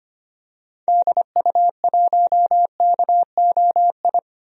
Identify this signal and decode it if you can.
Morse code, heard as a keyed tone, about 25 wpm, 700 Hz. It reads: DU1KOI
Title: DXCC-100 Call Signs - 6 Characters - 25wpm